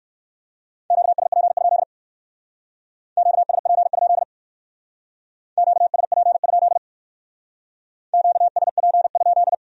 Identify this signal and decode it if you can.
Morse code, heard as a keyed tone, about 60 wpm, 700 Hz. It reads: QSP? QSP? QSP? QSP?